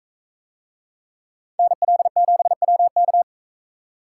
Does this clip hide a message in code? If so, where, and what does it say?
Yes: Morse code from 1.6 to 3.2 s, heard as a keyed tone, about 42 wpm, 700 Hz. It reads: NL7WK